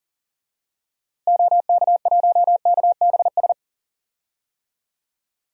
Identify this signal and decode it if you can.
Morse code, heard as a keyed tone, about 37 wpm, 700 Hz. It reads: OK1KBS
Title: DXCC-100 Call Signs - 6 Characters - Repeated 2x - 40wpm